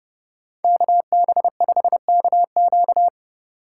Morse code, heard as a keyed tone, about 29 wpm, 700 Hz. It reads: KB5KQ